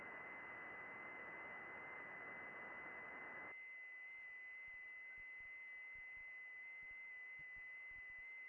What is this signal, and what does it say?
FT8.
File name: contests / june-vhf-k5tr- -0200